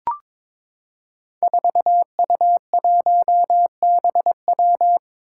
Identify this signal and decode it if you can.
Morse code, heard as a keyed tone, about 23 wpm, 700 Hz. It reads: 4U1BW
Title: Five Challenging Letter-Number Combinations 22wpm